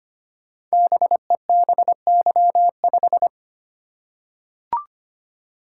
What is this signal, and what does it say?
Morse code, heard as a keyed tone, about 25 wpm, 700 Hz. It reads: BEBY5
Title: Code Groups - 5 Letter:Numbers - 25wpm